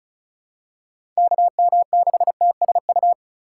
Morse code, heard as a keyed tone, about 34 wpm, 700 Hz. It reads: KM6TSU
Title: USA Call Signs - 6 Characters - 3x Character Spacing - 35wpm